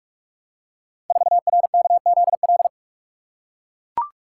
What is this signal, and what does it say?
Morse code, heard as a keyed tone, about 44 wpm, 700 Hz. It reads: 4RKZL
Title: Code Groups - 5 Letter:Numbers - 45wpm